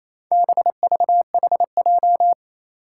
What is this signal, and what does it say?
Morse code, heard as a keyed tone, about 27 wpm, 700 Hz. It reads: BVHJ
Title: Four Challenging Letter-Number Combinations 28wpm